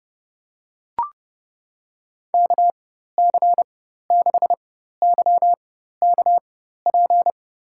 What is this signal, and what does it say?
Morse code, heard as a keyed tone, about 30 wpm, spaced as for 14 wpm, 700 Hz. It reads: KC6YKP
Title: USA Call Signs - 6 Characters - 4x Character Spacing - 30wpm